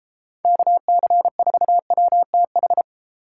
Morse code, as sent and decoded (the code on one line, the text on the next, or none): -.- -.-. ....- .-- - ....
KC4WTH